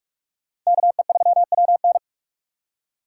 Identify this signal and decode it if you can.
Morse code, heard as a keyed tone, about 57 wpm, 700 Hz. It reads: KE3WN